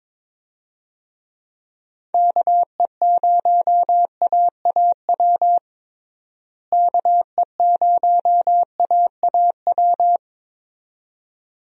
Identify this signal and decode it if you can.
Morse code, heard as a keyed tone, about 22 wpm, 700 Hz. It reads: KE0AAW KE0AAW
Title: USA Call Signs - 6 Characters - Repeated 2x - 22wpm